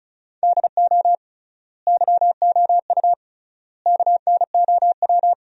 Morse code, heard as a keyed tone, about 34 wpm, 700 Hz. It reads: DO YOU KNOW
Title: 3x Word-Spacing - Sentences from Top 100 Words - 35wpm